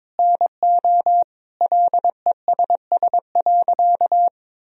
Morse code, heard as a keyed tone, about 22 wpm, 700 Hz. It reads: NO LESS.
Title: Sentences from Top 1000 Words - Review Part 2 - 22wpm